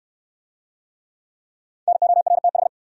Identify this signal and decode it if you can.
Morse code, heard as a keyed tone, about 66 wpm, 700 Hz. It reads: NORTH